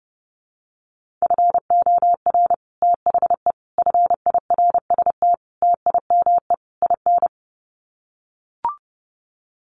Morse code, heard as a keyed tone, about 30 wpm, 700 Hz. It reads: FOR THE FIRST TIME IN
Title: Sets of 5 Words 30wpm